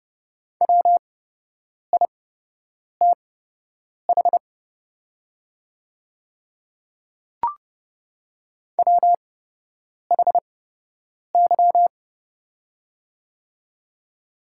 Morse code, heard as a keyed tone, about 30 wpm, spaced as for 8 wpm, 700 Hz. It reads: WITH WHY
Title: 8x Character Spacing - Top 200 Words - 30wpm